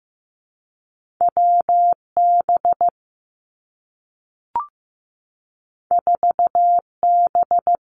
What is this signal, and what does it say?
Morse code, heard as a keyed tone, about 15 wpm, 700 Hz. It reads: W B 4 B
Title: Two Challenging Letter-Number Combinations 15wpm